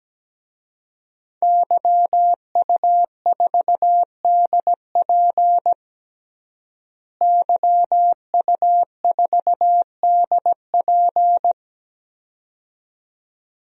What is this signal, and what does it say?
Morse code, heard as a keyed tone, about 17 wpm, 700 Hz. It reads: YU4DP YU4DP
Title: DXCC-100 Call Signs - 5 Characters - Repeated 2x - 17wpm